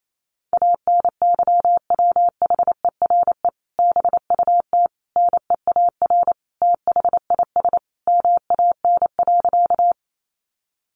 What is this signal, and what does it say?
Morse code, heard as a keyed tone, about 28 wpm, 700 Hz. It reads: ANYWHERE BUT NEAR THIS MAN.
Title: Sentences from Top 1400 Words 28wpm